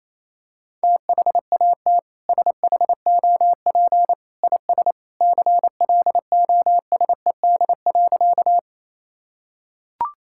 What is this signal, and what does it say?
Morse code, heard as a keyed tone, about 27 wpm, 700 Hz. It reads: THAT SHOP IS CLOSED.
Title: Sentences from Top 1600 Words 28wpm